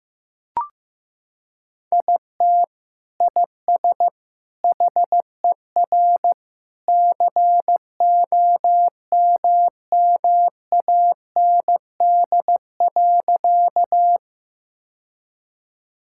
Morse code, heard as a keyed tone, about 15 wpm, 700 Hz. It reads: IT IS HER COMMAND.
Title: Sentences from Top 1600 Words 15wpm